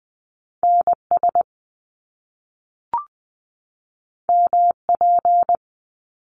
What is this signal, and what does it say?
Morse code, heard as a keyed tone, about 20 wpm, 700 Hz. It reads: N S M P
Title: Top Two Letter Combinations in Words 20wpm